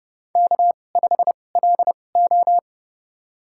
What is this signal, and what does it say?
Morse code, heard as a keyed tone, about 30 wpm, spaced as for 22 wpm, 700 Hz. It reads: K5LO
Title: USA Call Signs - 2x Character Spacing - 30wpm